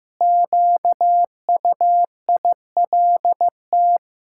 Morse code, heard as a keyed tone, about 15 wpm, 700 Hz. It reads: QUILT
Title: Pangram Challenge v2 - 15wpm